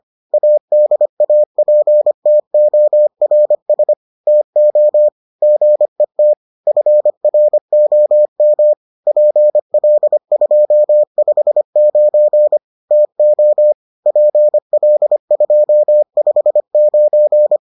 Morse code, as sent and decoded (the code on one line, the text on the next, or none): .- -.. .- .--. - --- .-. ... / - --- / --. - / ..-. .-. --- -- / .--. .-.. ..--- ..... ----. / - --- / .--. .-.. ..--- ..... ----.
ADAPTORS TO GET FROM PL259 TO PL259